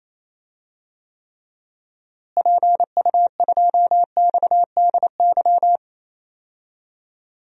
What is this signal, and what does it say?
Morse code, heard as a keyed tone, about 28 wpm, 700 Hz. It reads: PU2XDY
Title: DXCC-100 Call Signs - 6 Characters - Repeated 2x - 28wpm